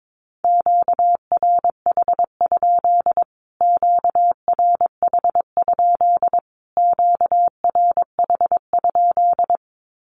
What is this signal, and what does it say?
Morse code, heard as a keyed tone, about 22 wpm, 700 Hz. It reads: QRH? QRH? QRH?
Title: Q-Codes Repeated 3x 22wpm